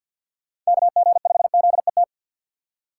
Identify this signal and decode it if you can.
Morse code, heard as a keyed tone, about 63 wpm, 700 Hz. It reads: KG5ZET